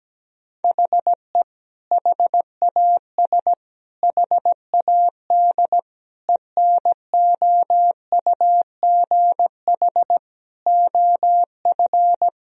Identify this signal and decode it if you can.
Morse code, heard as a keyed tone, about 17 wpm, 700 Hz. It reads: HE HAS HAD ENOUGH OF
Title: Sentences from Top 1000 Words - Review Part 3 - 17wpm